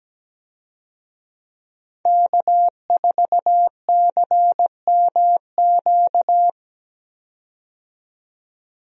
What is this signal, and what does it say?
Morse code, heard as a keyed tone, about 17 wpm, 700 Hz. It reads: K4CMQ
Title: USA Call Signs - 5 Characters - 17wpm